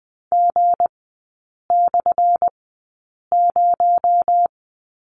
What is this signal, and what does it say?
Morse code, heard as a keyed tone, about 20 wpm, spaced as for 8 wpm, 700 Hz. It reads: G/0
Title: 218 Lesson 35e - Number 0 - Three Characters 20wpm